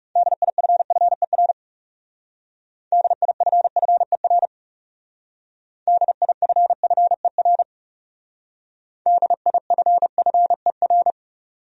Morse code, heard as a keyed tone, about 38 wpm, 700 Hz. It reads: DIFFER DIFFER DIFFER DIFFER